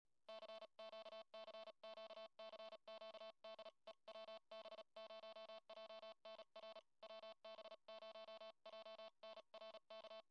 Morse code, heard as a keyed tone, about 34 wpm, 660 Hz, with noise. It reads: CQ CQ CQ DE WB0JNR WB0JNR K